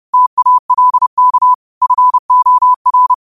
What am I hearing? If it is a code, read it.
Morse code, heard as a keyed tone, about 30 wpm, 1000 Hz. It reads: TALK FOR
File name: qotdfiles / 30wpm